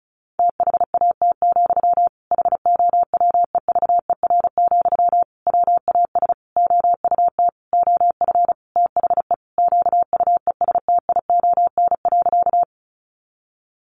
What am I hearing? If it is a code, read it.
Morse code, heard as a keyed tone, about 34 wpm, 700 Hz. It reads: THAT, HOWEVER, WAS OUT OF THE QUESTION.
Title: Sentences from Top 600 Words 35wpm